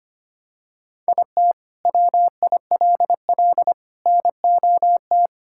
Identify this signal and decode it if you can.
Morse code, heard as a keyed tone, about 25 wpm, 700 Hz. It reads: IT WILL NOT
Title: Sentences from Top 1600 Words 25wpm